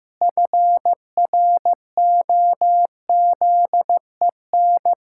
Morse code, heard as a keyed tone, about 15 wpm, 700 Hz. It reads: FROZEN
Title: Pangram Challenge v2 - 15wpm